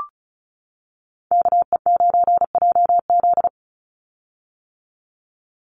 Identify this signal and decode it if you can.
Morse code, heard as a keyed tone, about 34 wpm, 700 Hz. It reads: KE9JZ
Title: USA Call Signs 35wpm